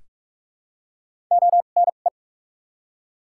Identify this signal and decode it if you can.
Morse code, heard as a keyed tone, about 46 wpm, spaced as for 32 wpm, 700 Hz. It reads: ONE
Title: 2x Character Spacing - Top 100 Words - 45wpm